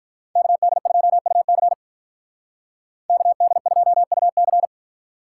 Morse code, heard as a keyed tone, about 43 wpm, 700 Hz. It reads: KD2UC KD2UC
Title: Speed Racing - USA Call Signs 35wpm